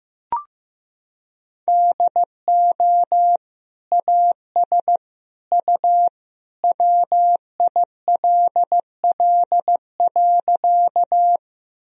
Morse code, heard as a keyed tone, about 15 wpm, 700 Hz. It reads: DO AS U WILL.
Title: Sentences from Top 100 Words and Abbreviations 15wpm